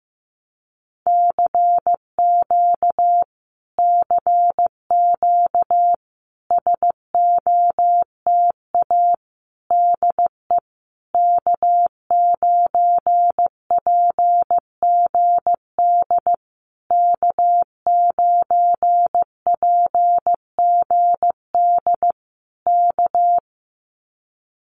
Morse code, heard as a keyed tone, about 15 wpm, 700 Hz. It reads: CQ CQ SOTA DE K9PGD K9PGD K
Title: Common QSO Phrases Encore 15wpm